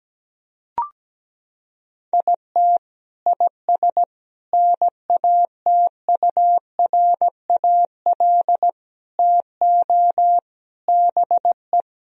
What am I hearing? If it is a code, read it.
Morse code, heard as a keyed tone, about 17 wpm, 700 Hz. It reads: IT IS NATURAL TO BE
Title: Sentences from Top 1000 Words - Review Part 4 - 17wpm